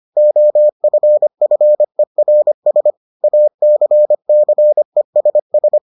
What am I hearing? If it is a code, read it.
Morse code, heard as a keyed tone, about 25 wpm, 600 Hz. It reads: OFFERS ACCESS